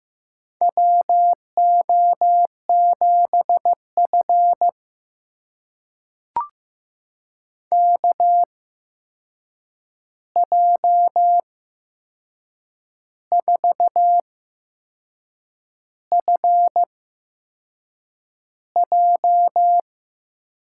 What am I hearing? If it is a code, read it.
Morse code, heard as a keyed tone, about 15 wpm, spaced as for 6 wpm, 700 Hz. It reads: WO7F KJ4FJ
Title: USA Call Signs - 8x Character Spacing - 15wpm